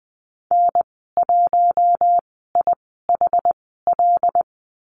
Morse code, heard as a keyed tone, about 20 wpm, spaced as for 14 wpm, 700 Hz. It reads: N1IHL
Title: 082 Lesson 14g - Number 2 - Call Signs 20wpm